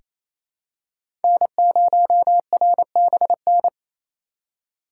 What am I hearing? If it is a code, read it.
Morse code, heard as a keyed tone, about 28 wpm, 700 Hz. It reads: N0RBN
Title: USA Call Signs - 5 Characters - 2x Character Spacing - 28wpm